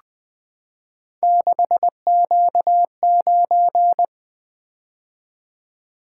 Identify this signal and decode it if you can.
Morse code, heard as a keyed tone, about 20 wpm, 700 Hz. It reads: 6Q9
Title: Three Letter-Number Combinations - 20wpm